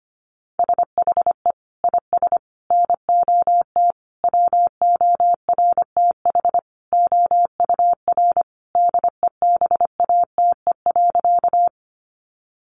Morse code, heard as a keyed tone, about 25 wpm, 700 Hz. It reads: SHE IS NOT WORTH OUR DEBATE.
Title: Sentences from Top 1100 Words 25wpm